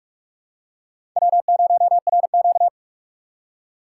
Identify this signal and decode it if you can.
Morse code, heard as a keyed tone, about 44 wpm, 700 Hz. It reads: W0RQ
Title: USA Call Signs - 4 Characters - 6x Character Spacing - 45wpm